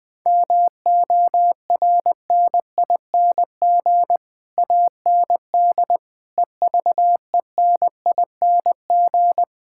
Morse code, heard as a keyed tone, about 19 wpm, 700 Hz. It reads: MORNING AND EVENING
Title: Sets of 9 Words - 20wpm